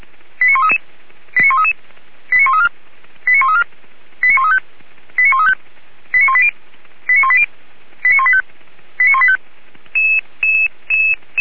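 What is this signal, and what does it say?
Five-tone selective call (ZVEI), heard as quick runs of five tones, tones of about 70 ms. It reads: 89133 89133 89134 89134 89135 89135 89180 89180 89185 89185
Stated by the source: ZVEI five-tone call received on SDR.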